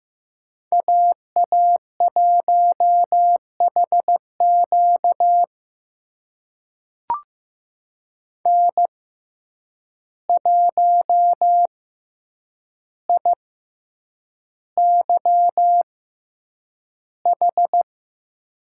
Morse code, heard as a keyed tone, about 15 wpm, spaced as for 8 wpm, 700 Hz. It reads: AA1HQ N1IYH
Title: USA Call Signs - 5 Characters - 6x Character Spacing - 15wpm